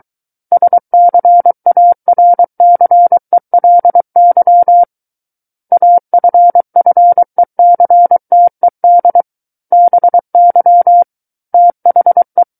Morse code, heard as a keyed tone, about 23 wpm, 700 Hz. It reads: SCARCELY AFFECTED BY THE
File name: Book One / 14-Chapter 14 - 23 WPM